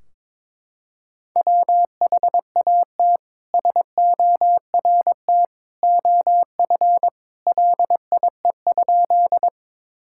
Morse code, heard as a keyed tone, about 23 wpm, 700 Hz. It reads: WHAT SORT OF LIE?
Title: Sentences from Top 1600 Words 22wpm